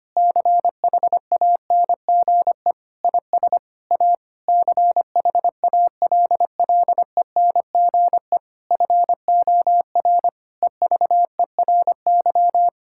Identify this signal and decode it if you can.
Morse code, heard as a keyed tone, about 25 wpm, 700 Hz. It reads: CHANGE IS A CHALLENGE FOR EVERY
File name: practice / 25wpm / Difficult Sentences with Hints - 25wpm